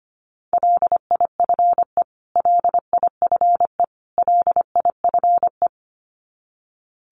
Morse code, heard as a keyed tone, about 25 wpm, 700 Hz. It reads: LIFE LIFE LIFE
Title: Top 500 Words Repeated 25wpm